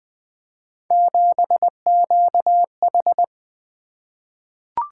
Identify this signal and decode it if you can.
Morse code, heard as a keyed tone, about 21 wpm, 700 Hz. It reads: 7QH
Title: Three Letter-Number Combinations - 20wpm